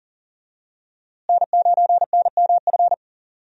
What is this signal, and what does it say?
Morse code, heard as a keyed tone, about 37 wpm, 700 Hz. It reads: N9NMF
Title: USA Call Signs - 5 Characters - Repeated 2x - 40wpm